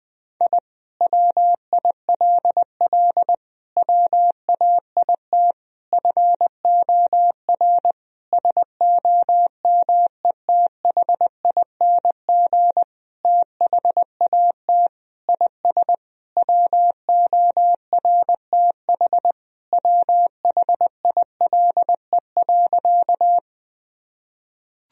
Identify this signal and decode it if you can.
Morse code, heard as a keyed tone, about 20 wpm, 700 Hz. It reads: I WILL WAIT FOR SOMETHING THAT IS WORTH WHILE.
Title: Sentences from Top 1000 Words - Review Part 1 - 20wpm